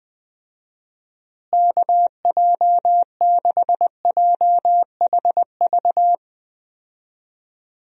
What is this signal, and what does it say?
Morse code, heard as a keyed tone, about 21 wpm, 700 Hz. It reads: KJ6JHV
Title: USA Call Signs - 6 Characters - Repeated 2x - 20wpm